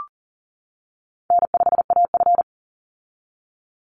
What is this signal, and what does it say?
Morse code, heard as a keyed tone, about 40 wpm, 700 Hz. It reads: N5AF
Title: Easy Call Signs - 40wpm